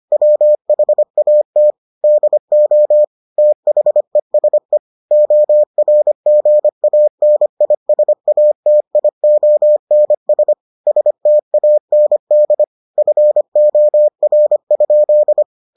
Morse code, heard as a keyed tone, about 25 wpm, 600 Hz. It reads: WHAT DO THESE ORGANISATIONS STAND FOR?